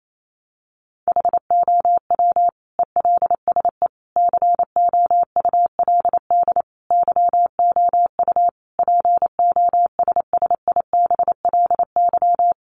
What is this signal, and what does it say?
Morse code, heard as a keyed tone, about 28 wpm, 700 Hz. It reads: HOW ELSE COULD YOU POSSIBLY
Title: Sentences from Top 1300 Words 28wpm